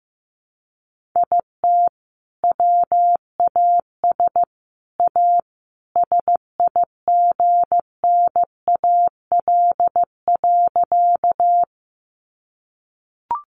Morse code, heard as a keyed tone, about 15 wpm, 700 Hz. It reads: IT WAS A SIGNAL.
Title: Sentences from Top 1400 Words 15wpm